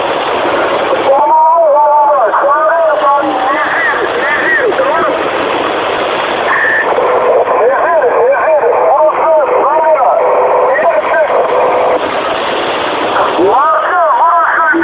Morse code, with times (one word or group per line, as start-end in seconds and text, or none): none